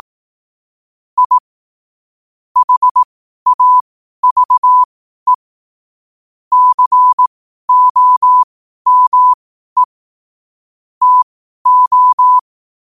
Morse code, heard as a keyed tone, about 18 wpm, spaced as for 12 wpm, 1000 Hz. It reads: I HAVE COME TO
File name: qotdfiles / 12wpm